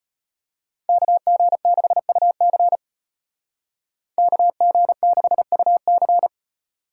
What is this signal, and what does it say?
Morse code, heard as a keyed tone, about 33 wpm, 700 Hz. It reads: KG6UC KG6UC